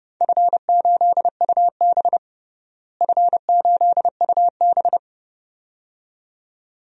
Morse code, heard as a keyed tone, about 30 wpm, 700 Hz. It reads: F8UB F8UB
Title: DXCC-100 Call Signs - 4 Characters - Repeated 2x - 30wpm